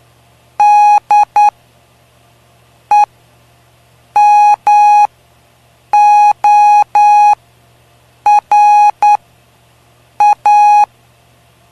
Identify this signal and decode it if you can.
Morse code, heard as a keyed tone, about 9 wpm, spaced as for 6 wpm, 830 Hz. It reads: DEMORA
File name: cw / py2lcd / 05-PY2LCD-Li��o 5